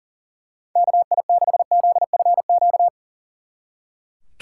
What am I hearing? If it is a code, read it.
Morse code, heard as a keyed tone, about 38 wpm, 700 Hz. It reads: KI6ZFQ
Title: Speed Racing - USA 2x3 Call Signs 40wpm